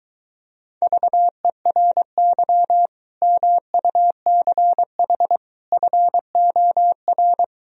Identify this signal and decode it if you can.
Morse code, heard as a keyed tone, about 24 wpm, 700 Hz. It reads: VERY MUCH FOR